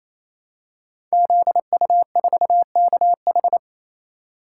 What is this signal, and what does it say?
Morse code, heard as a keyed tone, about 27 wpm, 700 Hz. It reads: ZU4KH
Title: Code Groups - 5 Letter:Numbers - 28wpm